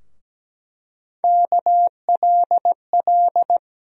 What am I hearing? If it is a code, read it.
Morse code, heard as a keyed tone, about 17 wpm, 700 Hz. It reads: KLL
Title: USA Call Sign Postfixes - 3 Letters 17wpm